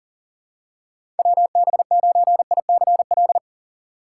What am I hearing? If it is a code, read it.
Morse code, heard as a keyed tone, about 40 wpm, 700 Hz. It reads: WB9ICL
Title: USA Call Signs - 6 Characters - 4x Character Spacing - 40wpm